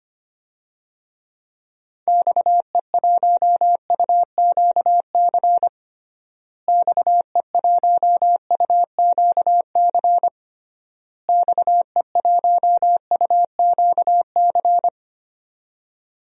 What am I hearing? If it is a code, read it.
Morse code, heard as a keyed tone, about 25 wpm, 700 Hz. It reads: XE1UQC XE1UQC XE1UQC
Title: DXCC-100 Call Signs - 6 Characters - Repeated 3x - 25wpm